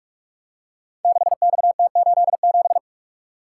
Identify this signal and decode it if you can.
Morse code, heard as a keyed tone, about 44 wpm, 690 Hz. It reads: 6XT87
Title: Code Groups - 5 Letter:Numbers - 45wpm